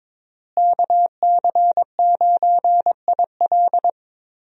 Morse code, heard as a keyed tone, about 22 wpm, 700 Hz. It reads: KC9IL
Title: USA Call Signs - 5 Characters - Repeated 2x - 22wpm